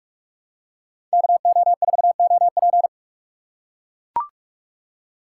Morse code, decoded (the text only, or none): KO4OP